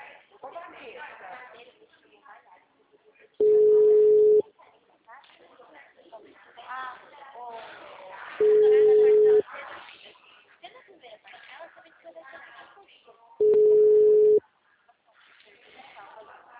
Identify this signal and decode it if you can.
Ringback tone (Europe, 425 Hz), with speech.
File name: out- -2006- - - .8143